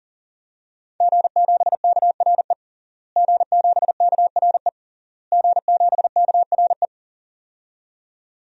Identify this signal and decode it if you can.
Morse code, heard as a keyed tone, about 40 wpm, 700 Hz. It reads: G7KRE G7KRE G7KRE
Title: DXCC-100 Call Signs - 5 Characters - Repeated 3x - 40wpm